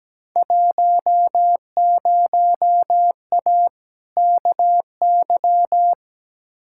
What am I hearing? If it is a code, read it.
Morse code, heard as a keyed tone, about 17 wpm, 700 Hz. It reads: 10A KY
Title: Field Day Exchanges Encore - 17wpm